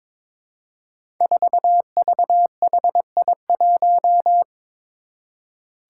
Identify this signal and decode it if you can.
Morse code, heard as a keyed tone, about 21 wpm, 700 Hz. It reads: 4VHI1
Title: Five Challenging Letter-Number Combinations 22wpm